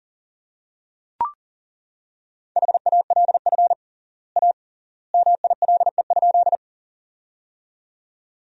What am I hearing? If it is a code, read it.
Morse code, heard as a keyed tone, about 37 wpm, 700 Hz. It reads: HALF A MILE?
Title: 3x Word-Spacing - Sentences from Top 400 Words - 40wpm